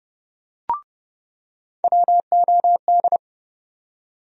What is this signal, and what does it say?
Morse code, heard as a keyed tone, about 31 wpm, 700 Hz. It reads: WOD
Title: USA Call Sign Postfixes - 3 Letters 30wpm